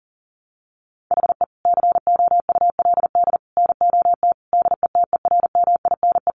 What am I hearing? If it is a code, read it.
Morse code, heard as a keyed tone, about 40 wpm, 700 Hz. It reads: HE COULD NOT DETERMINE
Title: Sentences from Top 700 Words 40wpm